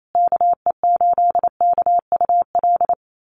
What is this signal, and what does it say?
Morse code, heard as a keyed tone, about 28 wpm, 700 Hz. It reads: KE8KUL
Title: USA Call Signs Repeated 3x 28wpm